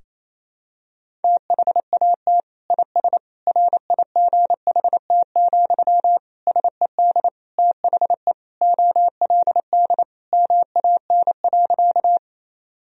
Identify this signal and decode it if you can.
Morse code, heard as a keyed tone, about 28 wpm, 700 Hz. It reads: THAT IS RIGHT, SED THE OLD MAN.
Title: Sentences from Top 200 Words and Abbreviations 28wpm